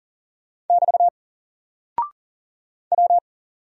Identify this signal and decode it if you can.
Morse code, heard as a keyed tone, about 39 wpm, 700 Hz. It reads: <BT> W
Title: Single Character (Letter:Number:Punctuation:Prosign) Alt 40wpm